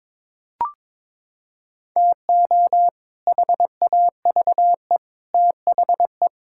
Morse code, heard as a keyed tone, about 22 wpm, 700 Hz. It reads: TO HAVE THE
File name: practice / 22wpm / Sentences with 3-Word Phrases - 22wpm